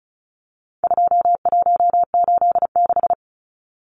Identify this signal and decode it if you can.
Morse code, heard as a keyed tone, about 34 wpm, 700 Hz. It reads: 2186
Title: Numbers - 4-Digit 35wpm